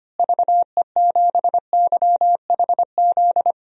Morse code, heard as a keyed tone, about 25 wpm, 700 Hz. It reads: VE7YHZ
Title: DXCC-100 Call Signs - 6 Characters - Repeated 3x - 25wpm